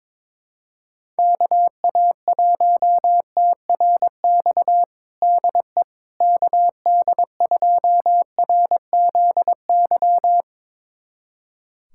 Morse code, heard as a keyed tone, about 22 wpm, 700 Hz. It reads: KA1TRX DE KD2RZY
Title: USA Callsign DE Callsign - 6 Characters - 22wpm